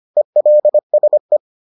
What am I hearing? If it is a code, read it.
Morse code, heard as a keyed tone, about 25 wpm, 600 Hz. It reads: ELSE